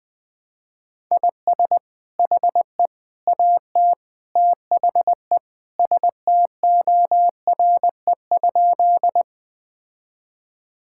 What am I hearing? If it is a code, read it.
Morse code, heard as a keyed tone, about 20 wpm, 700 Hz. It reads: IS HE AT THE STORE?
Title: Sentences from Top 1000 Words - Review Part 5 - 20wpm